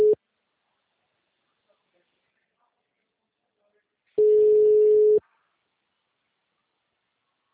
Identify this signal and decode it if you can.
Ringback tone (Europe, 425 Hz).
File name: out- -2015- - - .14